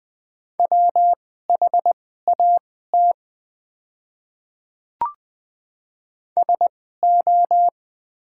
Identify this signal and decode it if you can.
Morse code, heard as a keyed tone, about 20 wpm, spaced as for 14 wpm, 700 Hz. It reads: WHAT SO
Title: 2x Character Spacing - Top 300 Words - 20wpm